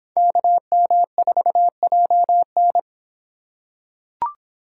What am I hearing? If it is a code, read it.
Morse code, heard as a keyed tone, about 26 wpm, 700 Hz. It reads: KM4JN